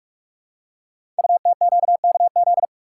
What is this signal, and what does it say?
Morse code, heard as a keyed tone, about 44 wpm, 700 Hz. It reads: UTQKZ
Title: Code Groups - 5 Letters - 45wpm